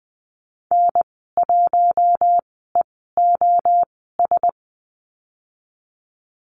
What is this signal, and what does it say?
Morse code, heard as a keyed tone, about 20 wpm, spaced as for 15 wpm, 700 Hz. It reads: N1EOS
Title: 042 Lesson 8g - Number 1 - Call Signs 20wpm